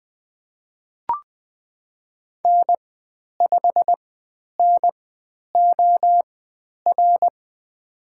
Morse code, heard as a keyed tone, about 20 wpm, spaced as for 10 wpm, 700 Hz. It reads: N5NOR
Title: 095 Lesson 16g - Letter c - Call Signs 20wpm